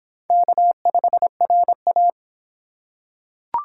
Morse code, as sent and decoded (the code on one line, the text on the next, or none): -.- ..... .-. .-
K5RA